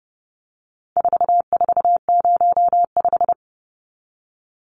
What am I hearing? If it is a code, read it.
Morse code, heard as a keyed tone, about 30 wpm, 700 Hz. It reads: 4405
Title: Numbers - 4-Digit 30wpm